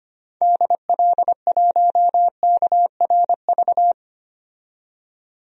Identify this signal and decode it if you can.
Morse code, heard as a keyed tone, about 25 wpm, 700 Hz. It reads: DL1KRV
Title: DXCC-100 Call Signs - 6 Characters - 25wpm